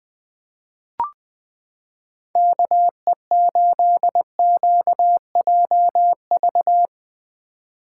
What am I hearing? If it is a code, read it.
Morse code, heard as a keyed tone, about 20 wpm, 700 Hz. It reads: KE8QJV
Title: 2023 POTA Activator Call Signs - 20wpm